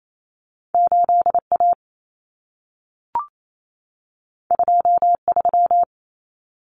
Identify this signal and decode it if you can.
Morse code, heard as a keyed tone, about 28 wpm, 700 Hz. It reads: 8A 23
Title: Common Cut Numbers - 2-Digit 28wpm